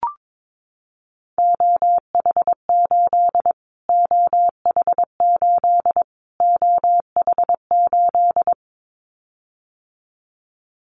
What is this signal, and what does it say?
Morse code, heard as a keyed tone, about 22 wpm, 700 Hz. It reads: OH8 OH8 OH8
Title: Callsign Prefixes for Top 25 DXCC - Repeated 3x - 22wpm